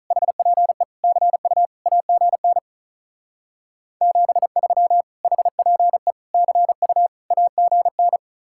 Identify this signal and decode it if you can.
Morse code, heard as a keyed tone, about 45 wpm, 700 Hz. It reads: HPE CU AGN 73 HPE CU AGN